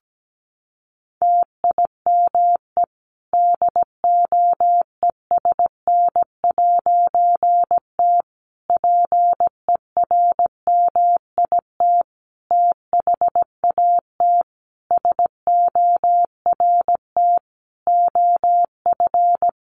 Morse code, heard as a keyed tone, about 17 wpm, 700 Hz. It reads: TIME DOESN'T PERMIT THAT SORT OF
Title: A Sound Of Thunder 17wpm